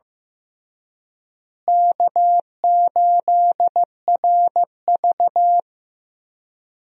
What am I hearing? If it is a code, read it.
Morse code, heard as a keyed tone, about 15 wpm, 700 Hz. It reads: K8RV
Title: USA Call Signs - 4 Characters - Repeated 2x - 15wpm